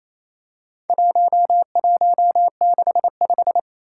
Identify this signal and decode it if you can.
Morse code, heard as a keyed tone, about 27 wpm, 700 Hz. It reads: 1165
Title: Four Challenging Letter-Number Combinations 28wpm